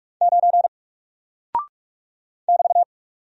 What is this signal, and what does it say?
Morse code, heard as a keyed tone, about 44 wpm, 700 Hz. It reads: 9 <BT>
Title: Single Character (Letter:Number:Punctuation:Prosign) 45wpm